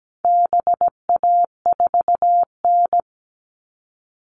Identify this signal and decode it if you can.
Morse code, heard as a keyed tone, about 17 wpm, 700 Hz. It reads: BA4N
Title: All Cut Numbers - 4-Digit 17wpm